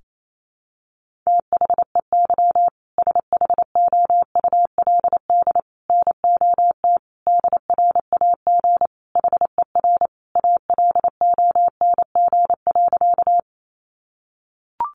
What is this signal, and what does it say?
Morse code, heard as a keyed tone, about 28 wpm, 700 Hz. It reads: THEY SHOULD NOT DRAG HER ALONG.
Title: Sentences from Top 1500 Words 28wpm